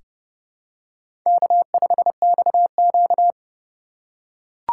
Morse code, heard as a keyed tone, about 30 wpm, 700 Hz. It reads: K5XQ
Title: K1USN SST QSO Practice - 30wpm